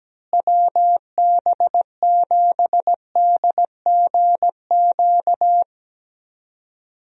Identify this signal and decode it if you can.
Morse code, heard as a keyed tone, about 17 wpm, 700 Hz. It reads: WB7DGQ
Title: USA Call Signs - 6 Characters - Repeated 2x - 17wpm